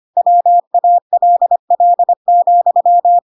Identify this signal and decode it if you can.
Morse code, heard as a keyed tone, about 25 wpm, 700 Hz. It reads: WALL,
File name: Pinocchio / Pinocchio ch4 25 25 ik1hge